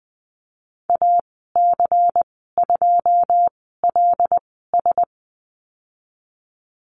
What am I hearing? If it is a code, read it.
Morse code, heard as a keyed tone, about 20 wpm, spaced as for 15 wpm, 700 Hz. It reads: AC2LS
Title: 154 Lesson 25g - Letter p - Call Signs 20wpm